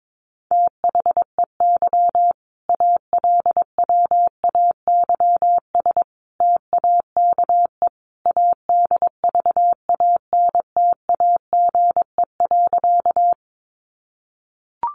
Morse code, heard as a keyed tone, about 22 wpm, 700 Hz. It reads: THEY ALWAYS TAKE ADVANTAGE.
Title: Sentences from Top 900 Words 22wpm